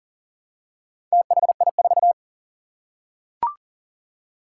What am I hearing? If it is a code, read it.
Morse code, heard as a keyed tone, about 41 wpm, 700 Hz. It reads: THI4
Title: Four Challenging Letter-Number Combinations 40wpm